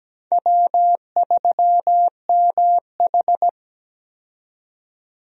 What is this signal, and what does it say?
Morse code, heard as a keyed tone, about 17 wpm, 700 Hz. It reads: W3MH
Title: USA Call Signs - 4 Characters - 17wpm